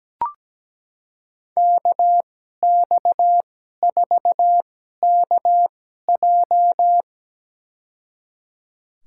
Morse code, heard as a keyed tone, about 17 wpm, spaced as for 12 wpm, 700 Hz. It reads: KX4KJ
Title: USA Call Signs - 5 Characters - 2x Character Spacing - 17wpm